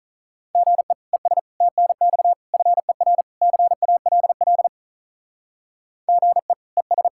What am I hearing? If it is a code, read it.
Morse code, heard as a keyed tone, about 46 wpm, 700 Hz. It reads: GE ES TNX FER CALL GE ES